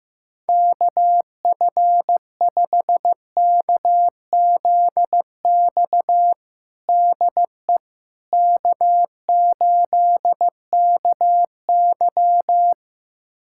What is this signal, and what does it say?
Morse code, heard as a keyed tone, about 15 wpm, 700 Hz. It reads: KF5KZX DE K8KY
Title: USA Callsign DE Callsign 15wpm